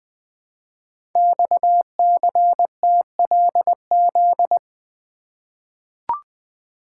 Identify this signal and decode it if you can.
Morse code, heard as a keyed tone, about 20 wpm, 700 Hz. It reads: XCTLZ
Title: Code Groups - 5 Letters - 20wpm